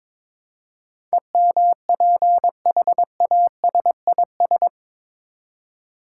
Morse code, heard as a keyed tone, about 21 wpm, 700 Hz. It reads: EMPHASIS
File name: practice / 22wpm / Eight Letter Words - 22wpm